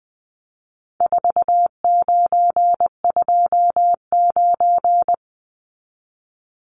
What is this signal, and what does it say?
Morse code, heard as a keyed tone, about 19 wpm, 700 Hz. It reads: 4929
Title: Numbers - 4-Digit 20wpm